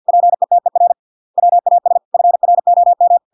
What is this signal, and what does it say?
Morse code, heard as a keyed tone, about 50 wpm, 700 Hz. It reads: PETER WAS FROM